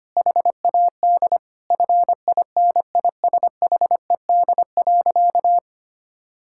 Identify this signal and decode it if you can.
Morse code, heard as a keyed tone, about 25 wpm, 700 Hz. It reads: HAD FINISHED.
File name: practice / 25wpm / Sentences with 3-Word Phrases - 25wpm